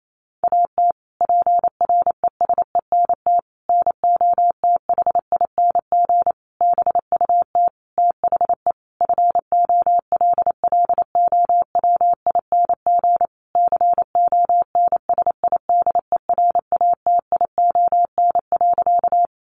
Morse code, heard as a keyed tone, about 28 wpm, 700 Hz. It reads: AT PRESENT NOTHING BUT THE FOLLOWING CONSIDERATION.
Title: Sentences from Top 1500 Words 28wpm